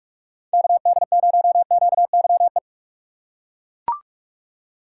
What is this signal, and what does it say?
Morse code, heard as a keyed tone, about 42 wpm, 700 Hz. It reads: KD0QYE
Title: USA Call Signs - 6 Characters - 2x Character Spacing - 45wpm